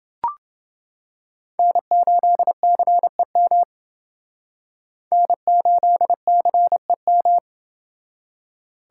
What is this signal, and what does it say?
Morse code, heard as a keyed tone, about 28 wpm, 700 Hz. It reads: N8CEM N8CEM